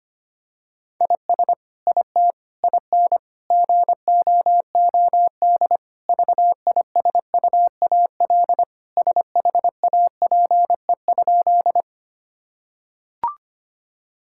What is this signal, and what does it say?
Morse code, heard as a keyed tone, about 25 wpm, 700 Hz. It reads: IS IT IN GOOD VISUAL SHAPE?
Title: Sentences from Top 1000 Words - Review Part 6 - 25wpm